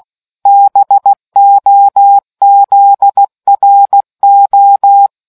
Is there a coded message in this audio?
Yes: Morse code from 0.5 to 5.1 s, heard as a keyed tone, about 16 wpm, 800 Hz. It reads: BOZRO